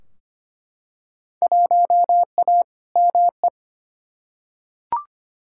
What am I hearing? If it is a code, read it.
Morse code, heard as a keyed tone, about 25 wpm, 700 Hz. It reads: 1A ME
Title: Field Day Exchanges Encore - 25wpm